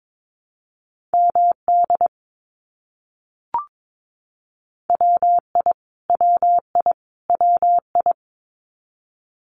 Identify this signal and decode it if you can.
Morse code, heard as a keyed tone, about 22 wpm, 700 Hz. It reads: MD WI WI WI
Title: US States Repeated 3x 22wpm